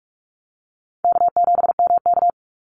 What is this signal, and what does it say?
Morse code, heard as a keyed tone, about 44 wpm, 700 Hz. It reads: K7MK
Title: NA SOTA Activator Callsigns 45wpm